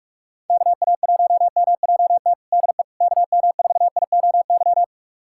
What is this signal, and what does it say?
Morse code, heard as a keyed tone, about 46 wpm, 700 Hz. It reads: KA1MJT DE KM4IOY
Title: USA Callsign DE Callsign - 6 Characters - 45wpm